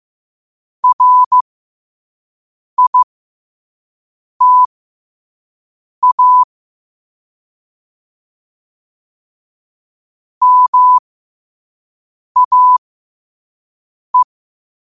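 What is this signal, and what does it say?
Morse code, heard as a keyed tone, about 15 wpm, spaced as for 5 wpm, 1000 Hz. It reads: RITA MAE
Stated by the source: Morse code at 5 wpm.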